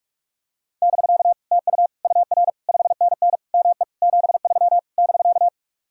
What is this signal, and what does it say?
Morse code, heard as a keyed tone, about 49 wpm, 700 Hz. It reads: <BK> TU UR 5NN ME 73 <BK>